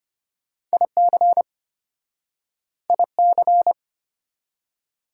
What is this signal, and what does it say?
Morse code, heard as a keyed tone, about 33 wpm, 700 Hz. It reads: IC IC